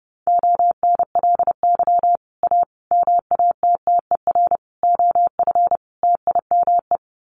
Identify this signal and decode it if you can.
Morse code, heard as a keyed tone, about 30 wpm, 700 Hz. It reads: ONLY A MATTER OF TIME
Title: Sets of 5 Words Encore 30wpm